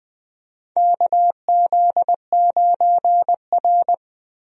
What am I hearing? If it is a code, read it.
Morse code, heard as a keyed tone, about 20 wpm, 700 Hz. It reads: KZ9R
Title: USA Call Signs - 4 Characters - Repeated 3x - 20wpm